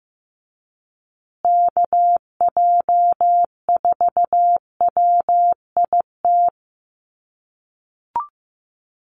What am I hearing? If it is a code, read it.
Morse code, heard as a keyed tone, about 15 wpm, 700 Hz. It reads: KJ4WIT
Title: USA Call Signs Repeated 3x 15wpm